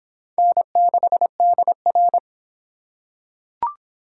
Morse code, heard as a keyed tone, about 23 wpm, 700 Hz. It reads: N6DR